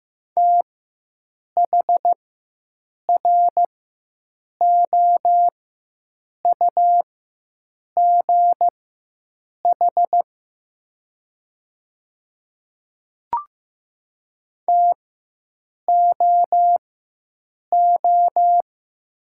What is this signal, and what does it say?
Morse code, heard as a keyed tone, about 15 wpm, spaced as for 7 wpm, 700 Hz. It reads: THROUGH TOO
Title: 4x Character Spacing - Top 300 Words - 15wpm